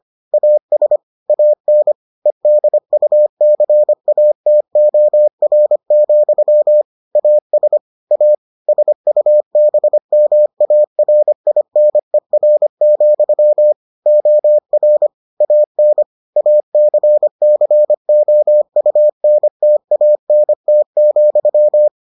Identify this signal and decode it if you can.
Morse code, heard as a keyed tone, about 25 wpm, 600 Hz. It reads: AS AN EDUCATOR, AS A SUBMARINER, OR AN ACCOUNTANT,